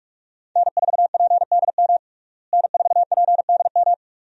Morse code, heard as a keyed tone, about 41 wpm, 700 Hz. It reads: N4PDM N4PDM
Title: USA Call Signs - 5 Characters - Repeated 2x - 45wpm